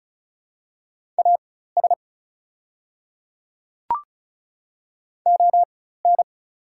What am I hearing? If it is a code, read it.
Morse code, heard as a keyed tone, about 36 wpm, spaced as for 16 wpm, 690 Hz. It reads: AS ON
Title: 4x Character Spacing - Top 100 Words - 35wpm